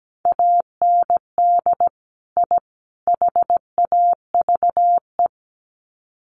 Morse code, heard as a keyed tone, about 17 wpm, 700 Hz. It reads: AND I HAVE
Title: Sets of 3 Words 17wpm